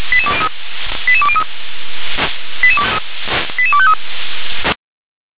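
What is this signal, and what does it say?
Five-tone selective call (ZVEI), heard as quick runs of five tones, tones of about 70 ms. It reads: 88203 88253
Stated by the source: ZVEI five-tone selective call received by SDR.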